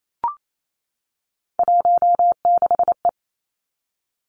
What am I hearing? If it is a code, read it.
Morse code, heard as a keyed tone, about 28 wpm, 700 Hz. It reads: 16E
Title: Three Challenging Letter-Number Combinations 28wpm